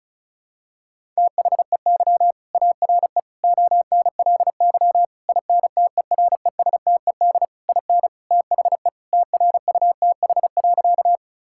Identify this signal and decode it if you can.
Morse code, heard as a keyed tone, about 36 wpm, 700 Hz. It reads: THEY ARE ONLY INTERESTED IN THE TRUTH.
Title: Sentences from Top 1000 Words - Review Part 6 - 35wpm